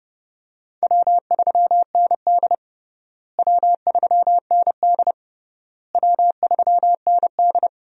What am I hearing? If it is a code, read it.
Morse code, heard as a keyed tone, about 30 wpm, 700 Hz. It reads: W3ND W3ND W3ND
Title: USA Call Signs - 4 Characters - Repeated 3x - 30wpm